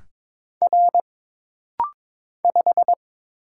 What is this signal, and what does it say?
Morse code, heard as a keyed tone, about 22 wpm, 700 Hz. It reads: R 5
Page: This rapid-fire timing is geared to push your ICR (Instant Character Recognition) skills.